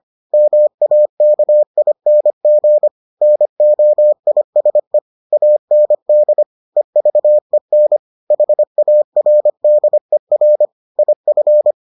Morse code, heard as a keyed tone, about 25 wpm, 600 Hz. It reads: MAKING NOISE AND EVEN HARDER IF